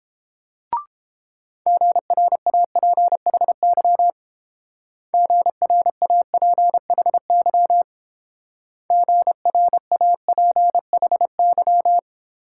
Morse code, heard as a keyed tone, about 28 wpm, 700 Hz. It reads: GRAPHY GRAPHY GRAPHY